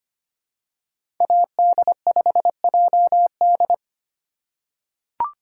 Morse code, heard as a keyed tone, about 25 wpm, 700 Hz. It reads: AD5JD
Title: USA Call Signs - 5 Characters - 2x Character Spacing - 25wpm